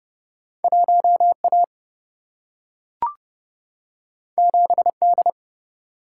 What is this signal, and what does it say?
Morse code, heard as a keyed tone, about 29 wpm, 700 Hz. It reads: 1 A 7 D
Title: Single-Letter Single-Number 30wpm